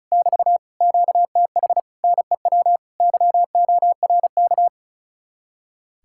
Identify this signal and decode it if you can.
Morse code, heard as a keyed tone, about 44 wpm, 700 Hz. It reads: <BT> QTH NEW YORK